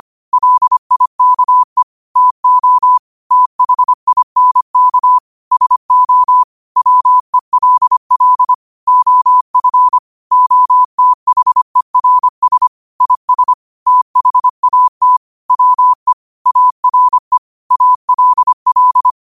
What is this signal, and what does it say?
Morse code, heard as a keyed tone, about 25 wpm, 1000 Hz. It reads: LIKE TO THINK SO WELL OF OTHERS IS THAT WE ARE ALL